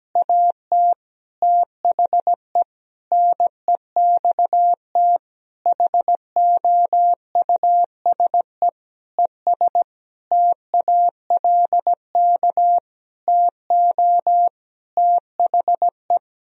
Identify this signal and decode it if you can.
Morse code, heard as a keyed tone, about 17 wpm, 700 Hz. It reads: AT THE NEXT HOUSE ES TALK TO THE
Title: Sentences from Top 400 Words and Abbreviations 17wpm